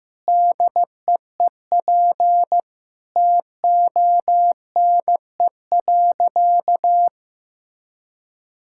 Morse code, heard as a keyed tone, about 15 wpm, 700 Hz. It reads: DEEP TONE.
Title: Sentences from Top 1600 Words 15wpm